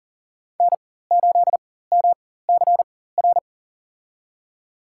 Morse code, heard as a keyed tone, about 40 wpm, spaced as for 19 wpm, 700 Hz. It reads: N8MCR
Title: USA Call Signs - 5 Characters - 4x Character Spacing - 40wpm